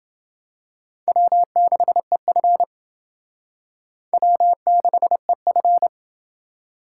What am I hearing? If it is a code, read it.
Morse code, heard as a keyed tone, about 29 wpm, 700 Hz. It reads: W6EF W6EF